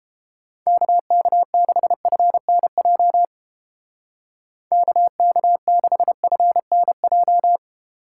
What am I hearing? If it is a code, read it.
Morse code, heard as a keyed tone, about 29 wpm, 700 Hz. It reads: KK6FNJ KK6FNJ